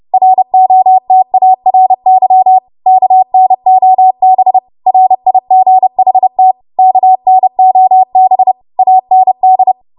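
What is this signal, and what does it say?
Morse code, heard as a keyed tone, about 30 wpm, 750 Hz. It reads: ROTARY KNOB RIGHT KNOB AND
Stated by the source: Morse practice transmission at 30 WPM.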